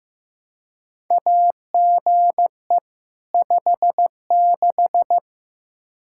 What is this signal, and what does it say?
Morse code, heard as a keyed tone, about 15 wpm, 700 Hz. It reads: AGE 56
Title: K1USN SST QSO Practice 15wpm